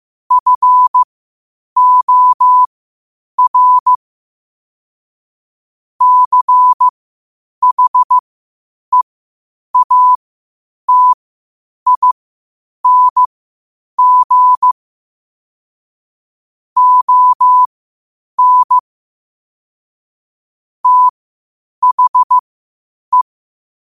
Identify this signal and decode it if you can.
Morse code, heard as a keyed tone, about 15 wpm, spaced as for 8 wpm, 1000 Hz. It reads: FOR CHEATING ON THE
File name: qotdfiles / 8wpm